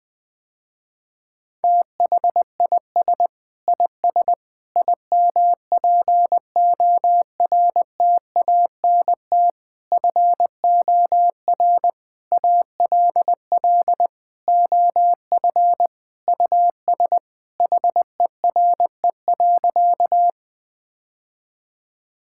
Morse code, heard as a keyed tone, about 20 wpm, 700 Hz. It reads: THIS IS IMPORTANT FOR ALL OF US HERE.
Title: Sentences with 5-Word Phrases - Spoken Beginning - 20wpm